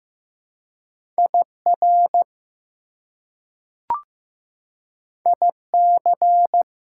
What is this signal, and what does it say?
Morse code, heard as a keyed tone, about 15 wpm, 700 Hz. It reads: IR IC
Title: Prefixes and Suffixes 15wpm